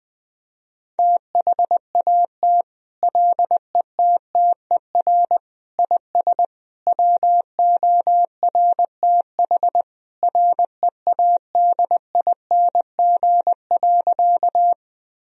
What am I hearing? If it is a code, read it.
Morse code, heard as a keyed tone, about 21 wpm, 700 Hz. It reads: THAT LETTER IS WORTH READING.
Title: Sentences from Top 1000 Words - Review Part 3 - 20wpm